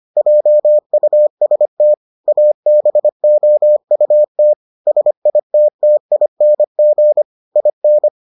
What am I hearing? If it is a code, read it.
Morse code, heard as a keyed tone, about 25 wpm, 600 Hz. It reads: JUST ABOUT SITTING IN